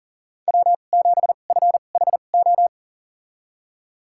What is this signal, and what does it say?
Morse code, heard as a keyed tone, about 38 wpm, spaced as for 28 wpm, 700 Hz. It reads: W7FHO
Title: USA Call Signs - 5 Characters - 2x Character Spacing - 40wpm